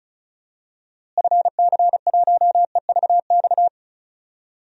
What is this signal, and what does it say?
Morse code, heard as a keyed tone, about 36 wpm, 700 Hz. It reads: FC1EVX
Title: DXCC-100 Call Signs - 6 Characters - Repeated 3x - 35wpm